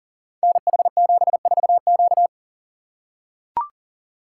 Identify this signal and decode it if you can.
Morse code, heard as a keyed tone, about 43 wpm, 700 Hz. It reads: NH74Q
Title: Code Groups - 5 Letter:Numbers - 40wpm